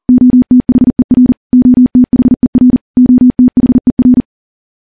Morse code, heard as a keyed tone, about 40 wpm, 250 Hz. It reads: OTHER OTHER OTHER
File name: morse / morse-training-40WPM